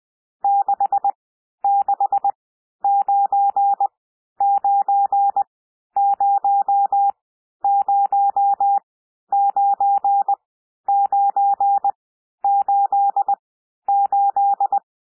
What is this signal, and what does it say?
Morse code, heard as a keyed tone, about 20 wpm, spaced as for 11 wpm, 800 Hz. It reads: 6699009988
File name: cwmp3 / N1 SOUND NUMBERS 20 wpm